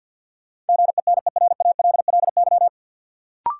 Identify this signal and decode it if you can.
Morse code, heard as a keyed tone, about 50 wpm, 700 Hz. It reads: GENERALLY
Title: Sentences from Top 600 Words - New Word Drill - 50wpm